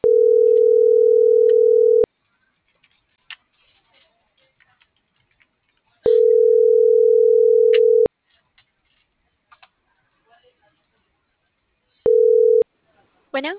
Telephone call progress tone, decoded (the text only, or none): ringback tone